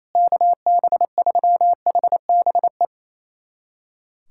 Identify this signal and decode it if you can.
Morse code, heard as a keyed tone, about 28 wpm, 700 Hz. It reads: KB3HBE
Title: USA Callsign DE Callsign 28wpm